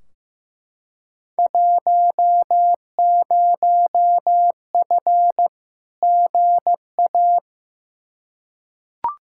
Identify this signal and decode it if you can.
Morse code, heard as a keyed tone, about 15 wpm, 700 Hz. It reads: 10F GA
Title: Field Day Exchanges Encore - 15wpm